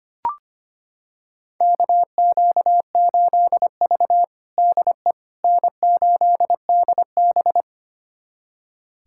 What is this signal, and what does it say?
Morse code, heard as a keyed tone, about 25 wpm, 700 Hz. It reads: KQ8V DE N8DB
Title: USA Callsign DE Callsign - 4 Characters - 25wpm